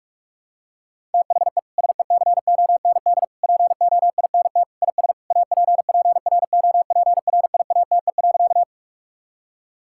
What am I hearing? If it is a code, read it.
Morse code, heard as a keyed tone, about 41 wpm, 700 Hz. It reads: THE SECOND POINT IS APPROPRIATE.
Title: Sentences from Top 1000 Words - Review Part 5 - 45wpm